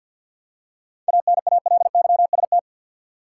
Morse code, heard as a keyed tone, about 47 wpm, 700 Hz. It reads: ANALYST
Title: Practice QSO - Exchange 3 - More Info - 50wpm